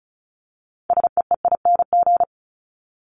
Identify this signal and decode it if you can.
Morse code, heard as a keyed tone, about 34 wpm, 700 Hz. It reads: SEEING
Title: ING Suffix - Top 100 Words with Base Words 35wpm